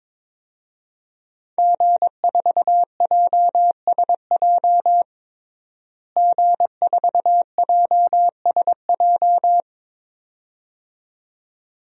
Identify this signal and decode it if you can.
Morse code, heard as a keyed tone, about 22 wpm, 700 Hz. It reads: G4JSJ G4JSJ
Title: DXCC-100 Call Signs - 5 Characters - Repeated 2x - 22wpm